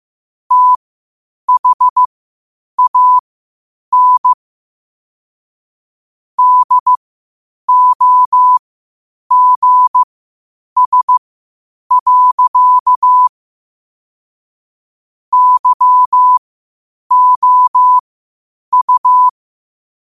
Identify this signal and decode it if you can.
Morse code, heard as a keyed tone, about 15 wpm, spaced as for 8 wpm, 1000 Hz. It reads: THAN DOGS. YOU
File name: qotdfiles / 8wpm